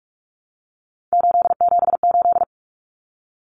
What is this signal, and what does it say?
Morse code, heard as a keyed tone, about 44 wpm, 700 Hz. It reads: 878
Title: Common Cut Numbers - 3-Digit 45wpm